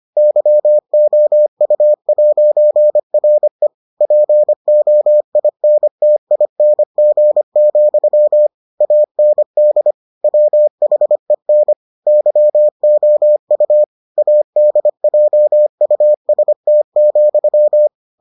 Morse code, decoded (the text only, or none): YOU'RE POINTING, AND WHEN YOU ADJUST,